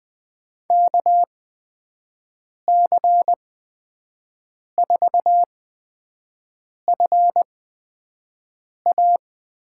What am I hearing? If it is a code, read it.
Morse code, heard as a keyed tone, about 20 wpm, spaced as for 5 wpm, 700 Hz. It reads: KC4FA
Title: USA Call Signs - 6 Characters - 8x Character Spacing - 20wpm